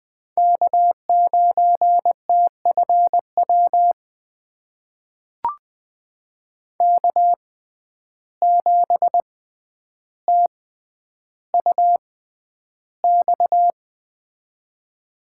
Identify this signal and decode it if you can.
Morse code, heard as a keyed tone, about 20 wpm, spaced as for 10 wpm, 700 Hz. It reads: K9TFW K7TUX
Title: USA Call Signs - 5 Characters - 6x Character Spacing - 20wpm